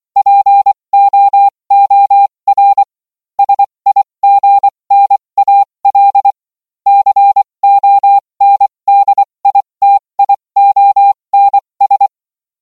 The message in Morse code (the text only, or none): POOR SIGNAL CONDITIONS